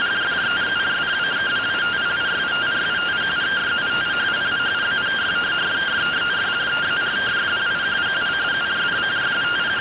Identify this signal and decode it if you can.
Two-tone teleprinter signal, about 170 Hz shift.